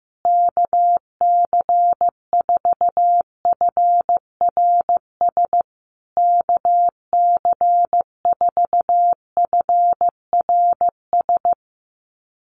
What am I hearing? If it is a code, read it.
Morse code, heard as a keyed tone, about 15 wpm, 700 Hz. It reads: KC4FRS KC4FRS
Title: USA Call Signs Repeated 3x 15wpm